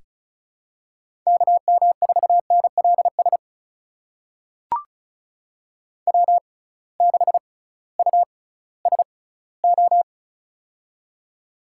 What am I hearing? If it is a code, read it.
Morse code, heard as a keyed tone, about 36 wpm, spaced as for 19 wpm, 700 Hz. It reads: KM4NLS W6USO
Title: USA Call Signs - 6x Character Spacing - 35wpm